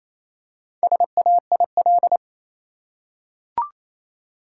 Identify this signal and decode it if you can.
Morse code, heard as a keyed tone, about 28 wpm, 700 Hz. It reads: SAIL
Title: Four Letter Words - 28wpm